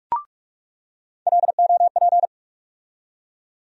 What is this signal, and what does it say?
Morse code, heard as a keyed tone, about 46 wpm, 700 Hz. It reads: LOP
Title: USA Call Sign Postfixes - 3 Letters 45wpm